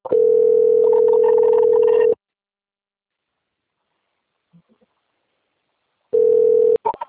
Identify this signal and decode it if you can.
Ringback tone (North America), 440 + 480 Hz.
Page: Cosmopolis - SP